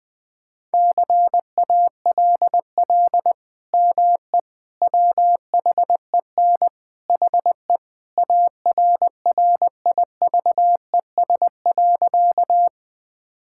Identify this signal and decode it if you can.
Morse code, heard as a keyed tone, about 20 wpm, 700 Hz. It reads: CALL ME WHEN HE ARRIVES.
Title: Sentences with 2-Word Phrases - 20wpm